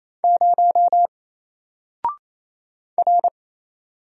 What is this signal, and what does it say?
Morse code, heard as a keyed tone, about 27 wpm, 700 Hz. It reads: 0 R
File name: Single-Letter-Number-28wpm